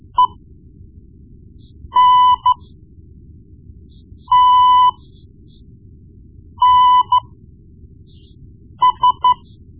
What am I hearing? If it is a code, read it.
Morse code, heard as a keyed tone, about 13 wpm, spaced as for 4 wpm, 1000 Hz. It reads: ENTNS
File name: Morse Lessons / Morse Lessons / Tape-1B